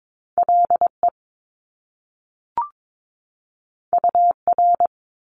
Morse code, heard as a keyed tone, about 22 wpm, 700 Hz. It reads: LE UR
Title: Top Two Letter Combinations in Words 22wpm